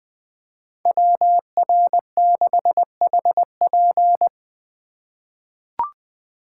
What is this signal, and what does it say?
Morse code, heard as a keyed tone, about 20 wpm, 700 Hz. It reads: WR6HP
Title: USA Call Signs - 5 Characters - 2x Character Spacing - 20wpm